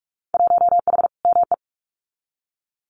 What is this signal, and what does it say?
Morse code, heard as a keyed tone, about 46 wpm, 700 Hz. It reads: 1H ME
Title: Winter Field Day Exchanges Encore 45wpm